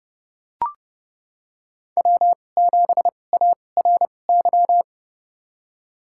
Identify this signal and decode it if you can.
Morse code, heard as a keyed tone, about 30 wpm, spaced as for 22 wpm, 690 Hz. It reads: W7ARY
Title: USA Call Signs - 5 Characters - 2x Character Spacing - 30wpm